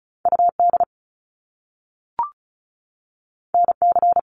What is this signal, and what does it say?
Morse code, heard as a keyed tone, about 36 wpm, 700 Hz. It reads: UD NC
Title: Top Two Letter Combinations in Words 35wpm